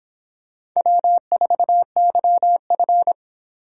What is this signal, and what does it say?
Morse code, heard as a keyed tone, about 26 wpm, 700 Hz. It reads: W4YF